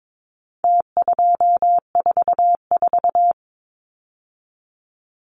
Morse code, heard as a keyed tone, about 22 wpm, 700 Hz. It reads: T244
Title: Numbers - Contest Style - 4-Digit 22wpm